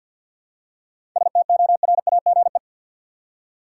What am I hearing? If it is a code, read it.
Morse code, heard as a keyed tone, about 53 wpm, 700 Hz. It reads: STORAGE
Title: Seven Letter Words - 50wpm